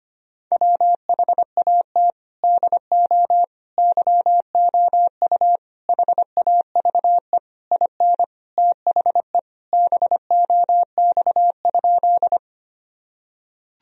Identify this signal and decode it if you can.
Morse code, heard as a keyed tone, about 25 wpm, 700 Hz. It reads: WHAT DO YOU HAVE IN THE BOX?
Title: Sentences from Top 1000 Words - Review Part 2 - 25wpm